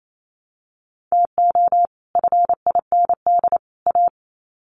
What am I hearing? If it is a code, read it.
Morse code, heard as a keyed tone, about 28 wpm, 700 Hz. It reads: TO FIND A
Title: Sets of 3 Words 28wpm